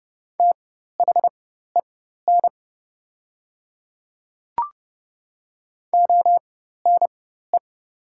Morse code, heard as a keyed tone, about 30 wpm, spaced as for 14 wpm, 700 Hz. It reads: THEN ONE
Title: 4x Character Spacing - Top 100 Words - 30wpm